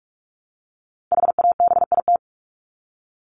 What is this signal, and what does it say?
Morse code, heard as a keyed tone, about 44 wpm, 700 Hz. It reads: HABIT
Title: Top 1751-2000 Words 45wpm